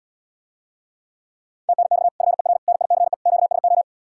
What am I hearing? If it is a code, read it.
Morse code, heard as a keyed tone, about 80 wpm, 700 Hz. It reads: TNX FER NICE QSO